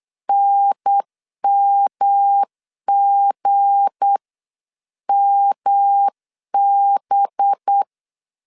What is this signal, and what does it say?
Morse code, heard as a keyed tone, about 8 wpm, 800 Hz. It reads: NMGMB
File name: CW / DET / 1 9 1500C S A 8wpm 5637